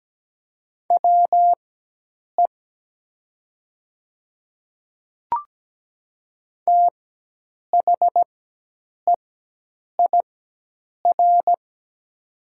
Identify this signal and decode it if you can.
Morse code, heard as a keyed tone, about 17 wpm, spaced as for 8 wpm, 700 Hz. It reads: WE THEIR
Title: 4x Character Spacing - Top 200 Words - 17wpm